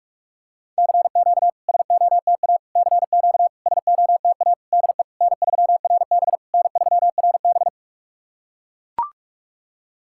Morse code, heard as a keyed tone, about 41 wpm, 690 Hz. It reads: CQ SOTA CQ SOTA DE N3RB N3RB
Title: SOTA QSO Practice 45wpm